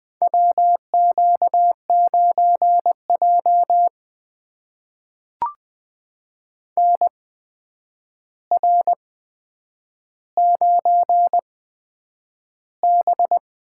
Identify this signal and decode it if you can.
Morse code, heard as a keyed tone, about 20 wpm, spaced as for 9 wpm, 700 Hz. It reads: WQ9J NR9B
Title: USA Call Signs - 4 Characters - 8x Character Spacing - 20wpm